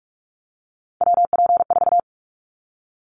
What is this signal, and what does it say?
Morse code, heard as a keyed tone, about 44 wpm, 700 Hz. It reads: WP4
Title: Callsign Prefixes for Top 25 DXCC - 45wpm